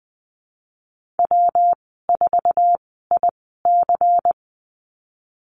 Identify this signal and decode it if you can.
Morse code, heard as a keyed tone, about 20 wpm, spaced as for 15 wpm, 700 Hz. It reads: W4IC
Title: 128 Lesson 21g - Number 6 - Call Signs 20wpm